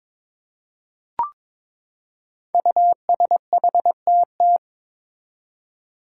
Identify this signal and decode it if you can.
Morse code, heard as a keyed tone, about 22 wpm, 700 Hz. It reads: USHTT
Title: Five Challenging Letter-Number Combinations 22wpm